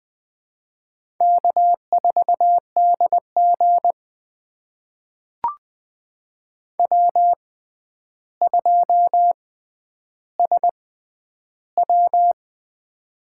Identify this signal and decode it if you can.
Morse code, heard as a keyed tone, about 21 wpm, spaced as for 10 wpm, 700 Hz. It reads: K4DG W2SW
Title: USA Call Signs - 6x Character Spacing - 20wpm